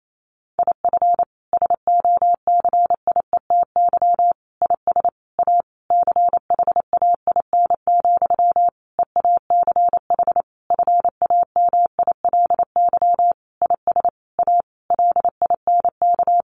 Morse code, heard as a keyed tone, about 28 wpm, 700 Hz. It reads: IF SOCIETY IS A CHAIN, EACH FAMILY IS A LINK
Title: Sentences from Top 1500 Words 28wpm